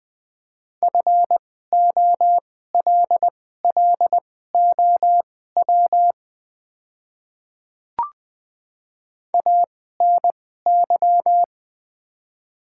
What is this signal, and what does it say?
Morse code, heard as a keyed tone, about 20 wpm, spaced as for 14 wpm, 700 Hz. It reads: FOLLOW ANY
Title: 2x Character Spacing - Top 300 Words - 20wpm